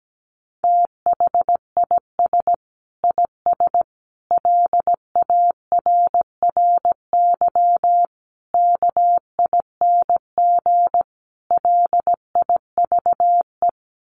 Morse code, heard as a keyed tone, about 17 wpm, 700 Hz. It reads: THIS IS LARRY KING LIVE
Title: Sets of 5 Words Encore2 17wpm